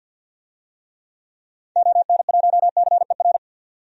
Morse code, heard as a keyed tone, about 52 wpm, 700 Hz. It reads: ON1CER
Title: DXCC-100 Call Signs - 6 Characters - 50wpm